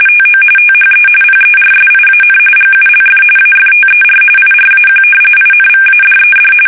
RTTY, about 45.45 baud. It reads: TO THE PERSIAN GULF AFTER IRAQ INVADED
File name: fsk850 / battleshipnj